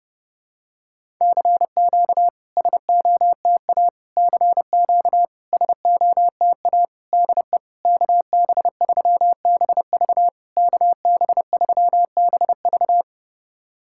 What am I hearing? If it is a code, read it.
Morse code, heard as a keyed tone, about 30 wpm, 700 Hz. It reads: CQ SOTA CQ SOTA DE KB3BV KB3BV
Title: SOTA QSO Practice 30wpm